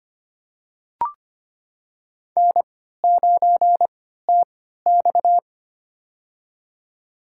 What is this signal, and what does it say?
Morse code, heard as a keyed tone, about 25 wpm, spaced as for 14 wpm, 700 Hz. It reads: N9TX
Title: USA Call Signs - 4 Characters - 3x Character Spacing - 25wpm